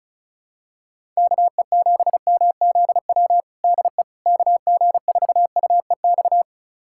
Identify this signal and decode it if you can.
Morse code, heard as a keyed tone, about 34 wpm, 700 Hz. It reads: KE7MZW DE KG4UEX
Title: USA Callsign DE Callsign - 6 Characters - 35wpm